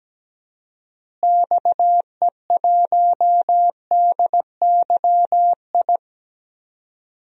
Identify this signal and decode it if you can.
Morse code, heard as a keyed tone, about 17 wpm, 700 Hz. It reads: XE1DYI
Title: DXCC-100 Call Signs - 6 Characters - Repeated 3x - 17wpm